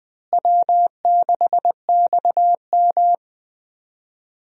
Morse code, heard as a keyed tone, about 20 wpm, 700 Hz. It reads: W6XM
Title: USA Call Signs - 4 Characters - Repeated 2x - 20wpm